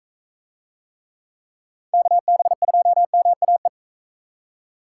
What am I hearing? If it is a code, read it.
Morse code, heard as a keyed tone, about 45 wpm, 700 Hz. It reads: KB2MAE